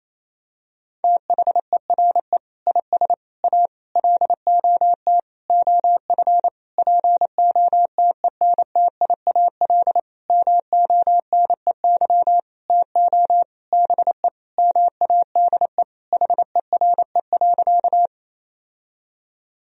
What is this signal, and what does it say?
Morse code, heard as a keyed tone, about 28 wpm, 700 Hz. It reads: THERE IS A LOT OF POTENTIAL MONEY TO BE MADE HERE.
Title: Sentences from Top 1000 Words - Review Part 4 - 28wpm